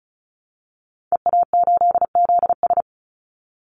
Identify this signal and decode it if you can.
Morse code, heard as a keyed tone, about 34 wpm, 700 Hz. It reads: EA8ZS
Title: Call Signs Repeated 2x - 35wpm